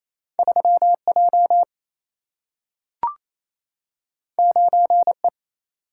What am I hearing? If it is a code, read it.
Morse code, heard as a keyed tone, about 28 wpm, 700 Hz. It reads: 3 J 9 E
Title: Single-Letter Single-Number 28wpm